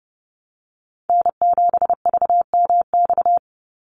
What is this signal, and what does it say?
Morse code, heard as a keyed tone, about 30 wpm, 700 Hz. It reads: N7VMX
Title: USA Call Signs 30wpm